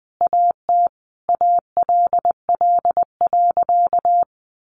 Morse code, heard as a keyed tone, about 20 wpm, 700 Hz. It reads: AT ALL.
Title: Sentences from Top 1200 Words 20wpm